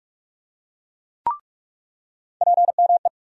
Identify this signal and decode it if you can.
Morse code, heard as a keyed tone, about 46 wpm, 700 Hz. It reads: PME
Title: USA Call Sign Postfixes - 3 Letters 45wpm